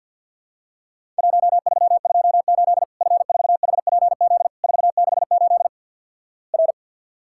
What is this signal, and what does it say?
Morse code, heard as a keyed tone, about 54 wpm, 700 Hz. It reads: 1328 F4HPZ 468 R
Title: Worked All Europe DX Contest - QTC Exchange - 50wpm